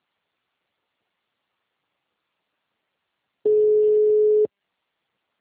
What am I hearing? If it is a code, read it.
Ringback tone (Europe, 425 Hz).